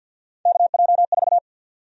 Morse code, heard as a keyed tone, about 53 wpm, 700 Hz. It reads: KJ4
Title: USA Call Sign Prefixes - 50wpm